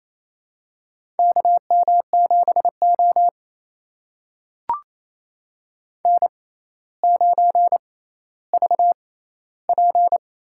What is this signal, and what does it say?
Morse code, heard as a keyed tone, about 28 wpm, spaced as for 14 wpm, 700 Hz. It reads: KM7O N9VP
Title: USA Call Signs - 6x Character Spacing - 28wpm